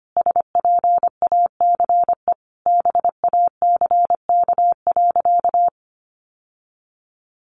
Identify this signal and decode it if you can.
Morse code, heard as a keyed tone, about 25 wpm, 700 Hz. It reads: SPACE BACK.
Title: Sentences from Top 1500 Words 25wpm